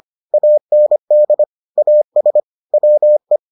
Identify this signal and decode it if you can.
Morse code, heard as a keyed tone, about 25 wpm, 600 Hz. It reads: AND AS WE